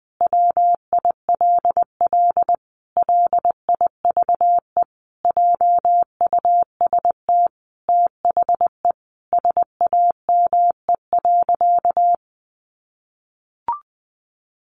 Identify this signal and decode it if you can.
Morse code, heard as a keyed tone, about 20 wpm, 700 Hz. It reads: WILL LIVE JUST THE SAME.
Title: Sentences from Top 300 Words 20wpm